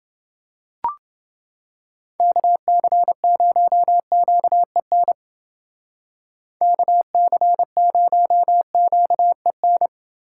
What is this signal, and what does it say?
Morse code, heard as a keyed tone, about 29 wpm, 700 Hz. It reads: KC0QEN KC0QEN